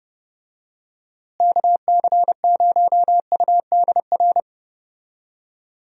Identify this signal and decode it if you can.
Morse code, heard as a keyed tone, about 32 wpm, 700 Hz. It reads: KC0UDR
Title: USA Call Signs - 6 Characters - Repeated 2x - 30wpm